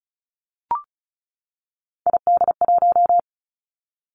Morse code, heard as a keyed tone, about 36 wpm, 700 Hz. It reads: ID1
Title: Three Challenging Letter-Number Combinations 35wpm